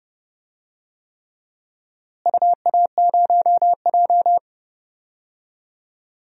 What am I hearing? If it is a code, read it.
Morse code, heard as a keyed tone, about 30 wpm, 700 Hz. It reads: UA0J
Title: DXCC-100 Call Signs - 4 Characters - Repeated 3x - 30wpm